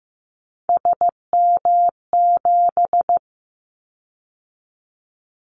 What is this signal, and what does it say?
Morse code, heard as a keyed tone, about 15 wpm, 700 Hz. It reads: SM7
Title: Callsign Prefixes for Top 25 DXCC - 15wpm